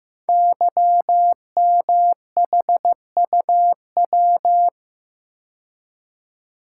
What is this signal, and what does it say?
Morse code, heard as a keyed tone, about 15 wpm, 700 Hz. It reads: YMHUW
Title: Code Groups - 5 Letter:Numbers - 15wpm